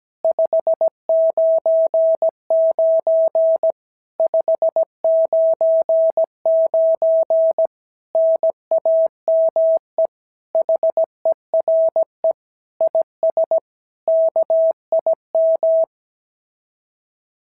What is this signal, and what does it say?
Morse code, heard as a keyed tone, about 17 wpm, 650 Hz. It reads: 599 599 NAME HERE IS KIM
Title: SOTA QSO Practice 17wpm